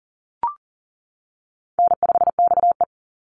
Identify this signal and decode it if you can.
Morse code, heard as a keyed tone, about 40 wpm, 700 Hz. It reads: N5XE
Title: Easy Call Signs - 40wpm